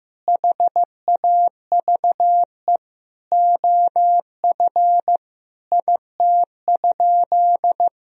Morse code, heard as a keyed tone, about 15 wpm, 700 Hz. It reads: HAVE OF IT?
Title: Sentences from Top 1000 Words - Review Part 3 - 15wpm